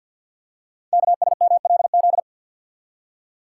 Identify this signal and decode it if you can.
Morse code, heard as a keyed tone, about 48 wpm, 690 Hz. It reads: KSMLZ
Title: Code Groups - 5 Letters - 50wpm